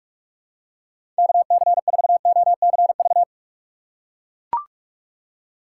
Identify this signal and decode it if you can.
Morse code, heard as a keyed tone, about 43 wpm, 700 Hz. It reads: KC4OCV
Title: USA Call Signs - 6 Characters - 6x Character Spacing - 45wpm